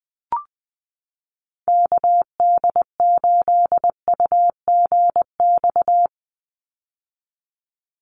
Morse code, heard as a keyed tone, about 20 wpm, 700 Hz. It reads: KD8UGX
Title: USA Call Signs 20wpm